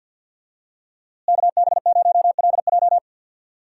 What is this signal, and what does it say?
Morse code, heard as a keyed tone, about 53 wpm, 700 Hz. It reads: KB0LJ
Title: USA Call Signs - 5 Characters - Repeated 2x - 50wpm